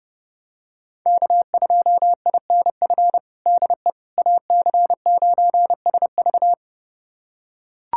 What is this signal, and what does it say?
Morse code, heard as a keyed tone, about 30 wpm, 700 Hz. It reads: K2INF DE AC9SV
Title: USA Callsign DE Callsign - 5 Characters - 30wpm